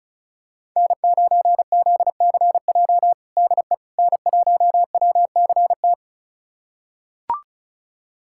Morse code, heard as a keyed tone, about 36 wpm, 700 Hz. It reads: N9ZCJ DE N1WCT
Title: USA Callsign DE Callsign - 5 Characters - 35wpm